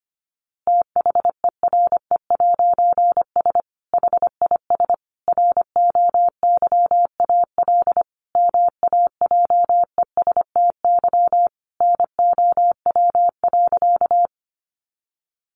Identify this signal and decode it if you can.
Morse code, heard as a keyed tone, about 25 wpm, 700 Hz. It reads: THERE'S HIS ROYAL MAJESTY NOW.
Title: A Sound Of Thunder 25wpm